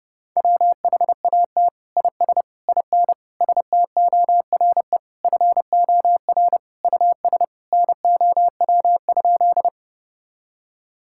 Morse code, heard as a keyed tone, about 32 wpm, 700 Hz. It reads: WHAT IS IN STORE FOR US NOW?
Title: Sentences from Top 1000 Words - Review Part 5 - 30wpm